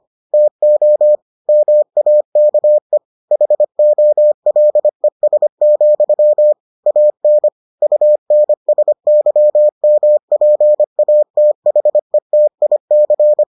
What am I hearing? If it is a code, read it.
Morse code, heard as a keyed tone, about 25 wpm, 600 Hz. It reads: TO MAKE HOLES, AN UNSYMPATHETIC